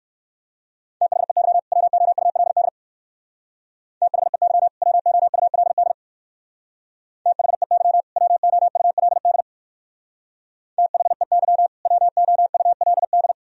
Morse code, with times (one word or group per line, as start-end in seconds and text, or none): 1.0-1.6 THEY
1.7-2.7 WOULD
4.0-4.7 THEY
4.8-5.9 WOULD
7.3-8.0 THEY
8.2-9.4 WOULD
10.8-11.7 THEY
11.9-13.3 WOULD